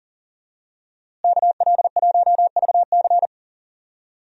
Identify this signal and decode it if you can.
Morse code, heard as a keyed tone, about 40 wpm, 700 Hz. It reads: KL1VC
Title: USA Call Signs - 5 Characters - Repeated 2x - 40wpm